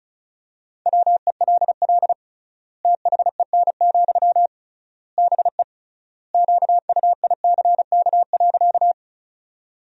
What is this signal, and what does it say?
Morse code, heard as a keyed tone, about 34 wpm, 700 Hz. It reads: WELL THEN, BE QUICK.
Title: 3x Word-Spacing - Sentences from Top 500 Words - 35wpm